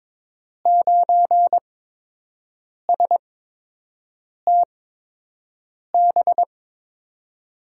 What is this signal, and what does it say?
Morse code, heard as a keyed tone, about 22 wpm, spaced as for 6 wpm, 700 Hz. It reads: 9STB
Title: USA Call Signs - 8x Character Spacing - 22wpm